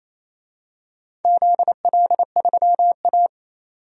Morse code, heard as a keyed tone, about 28 wpm, 700 Hz. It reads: ZL3A
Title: DXCC-100 Call Signs - 4 Characters - 28wpm